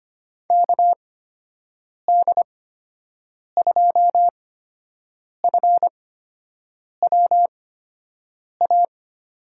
Morse code, heard as a keyed tone, about 25 wpm, spaced as for 7 wpm, 700 Hz. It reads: KD2FWA
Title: USA Call Signs - 6 Characters - 8x Character Spacing - 25wpm